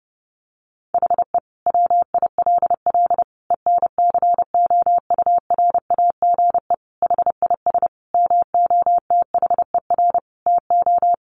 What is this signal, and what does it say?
Morse code, heard as a keyed tone, about 30 wpm, 700 Hz. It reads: HE WILL ENCOURAGE HIS MOTHER TO
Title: Sentences from Top 1400 Words 30wpm